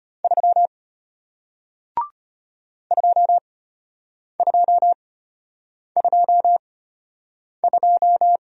Morse code, heard as a keyed tone, about 33 wpm, spaced as for 8 wpm, 700 Hz. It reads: 3 2222